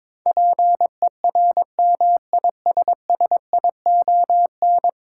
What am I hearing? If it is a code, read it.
Morse code, heard as a keyed tone, about 22 wpm, 700 Hz. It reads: PERMISSION
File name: practice / 22wpm / 10 Letter Words - First 3 Letters Given - 22wpm